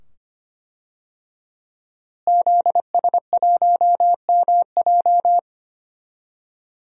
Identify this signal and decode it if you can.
Morse code, heard as a keyed tone, about 25 wpm, 700 Hz. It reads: ZS1MJ
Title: DXCC-100 Call Signs - 5 Characters - Repeated 2x - 25wpm